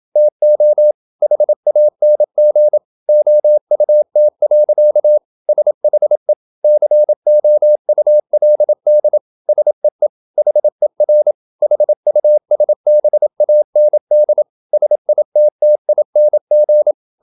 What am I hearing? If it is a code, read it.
Morse code, heard as a keyed tone, about 27 wpm, 600 Hz. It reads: TO HANG OUT. SHE COULD SEE HER HUSBAND SITTING
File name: Never 0027wpm